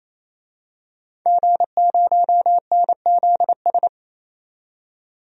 Morse code, heard as a keyed tone, about 28 wpm, 700 Hz. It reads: G0NZS
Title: DXCC-100 Call Signs - 5 Characters - 28wpm